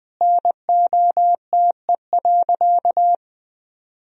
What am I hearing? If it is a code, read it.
Morse code, heard as a keyed tone, about 20 wpm, 700 Hz. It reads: NOTE.
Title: Sentences from Top 1000 Words - Review Part 4 - 20wpm